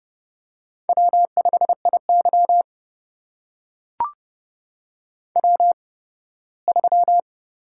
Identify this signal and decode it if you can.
Morse code, heard as a keyed tone, about 31 wpm, spaced as for 18 wpm, 700 Hz. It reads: W5IY W3
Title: USA Call Signs - 4 Characters - 8x Character Spacing - 30wpm